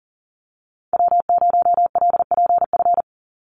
Morse code, heard as a keyed tone, about 40 wpm, 700 Hz. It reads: W0LPF
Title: Call Signs Repeated 3x - 40wpm